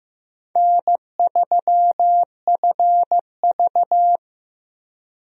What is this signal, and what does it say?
Morse code, heard as a keyed tone, about 15 wpm, 700 Hz. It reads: N3FV
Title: USA Call Signs - 4 Characters - Repeated 3x - 15wpm